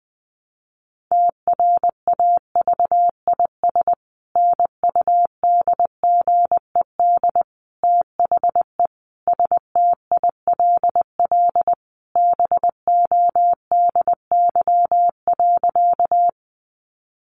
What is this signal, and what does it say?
Morse code, heard as a keyed tone, about 20 wpm, 700 Hz. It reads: TRAVIS NUDGED THE STILL BODY.
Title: A Sound Of Thunder 20wpm